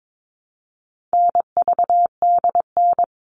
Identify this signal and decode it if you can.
Morse code, heard as a keyed tone, about 22 wpm, 700 Hz. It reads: NVDN
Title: All Cut Numbers - 4-Digit 22wpm